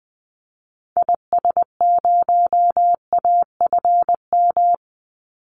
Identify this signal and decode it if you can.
Morse code, heard as a keyed tone, about 20 wpm, 700 Hz. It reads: IS0AFM
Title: Call Signs - 20wpm